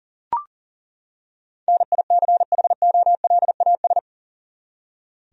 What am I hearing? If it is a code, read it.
Morse code, heard as a keyed tone, about 40 wpm, 700 Hz. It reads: NICHOLAS
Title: USA Common Names - 40wpm